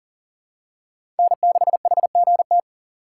Morse code, heard as a keyed tone, about 38 wpm, 700 Hz. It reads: N6HGT
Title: USA Call Signs - 5 Characters - 3x Character Spacing - 40wpm